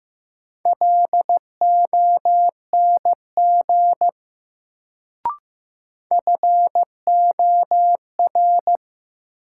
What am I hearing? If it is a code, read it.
Morse code, heard as a keyed tone, about 15 wpm, 700 Hz. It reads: LONG FOR
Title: Sentences from Top 100 Words - New Word Drill - 15wpm